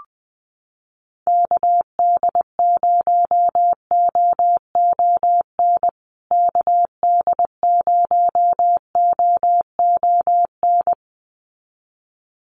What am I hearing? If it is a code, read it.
Morse code, heard as a keyed tone, about 20 wpm, 700 Hz. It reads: KD0OON KD0OON
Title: USA Call Signs Repeated 2x 20wpm